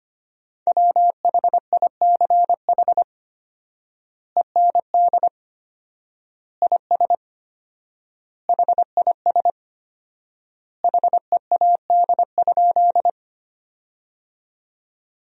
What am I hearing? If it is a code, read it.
Morse code, heard as a keyed tone, about 25 wpm, 700 Hz. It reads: WHICH END IS HIS HEAD?
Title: 4x Word-Spacing - Sentences from Top 300 Words - 25wpm